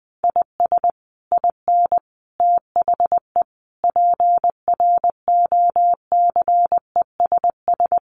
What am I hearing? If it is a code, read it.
Morse code, heard as a keyed tone, about 20 wpm, 700 Hz. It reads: IS IN THE PROCESS
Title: Sets of 5 Words Encore2 20wpm